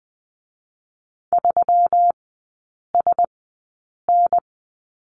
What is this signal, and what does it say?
Morse code, heard as a keyed tone, about 20 wpm, spaced as for 8 wpm, 700 Hz. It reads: 3SN
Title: 121 Lesson 20e - Number 3 - Three Characters 20wpm